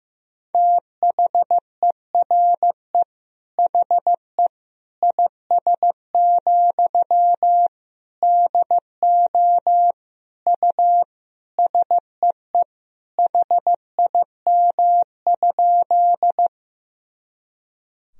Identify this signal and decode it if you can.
Morse code, heard as a keyed tone, about 16 wpm, 700 Hz. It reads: THERE HE IS, DO U SEE HIM?
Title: Sentences from Top 100 Words and Abbreviations 15wpm